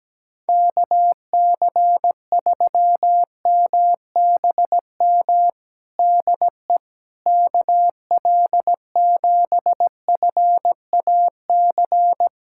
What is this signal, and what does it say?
Morse code, heard as a keyed tone, about 17 wpm, 700 Hz. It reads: KC3MBM DE KL7FAC
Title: USA Callsign DE Callsign 17wpm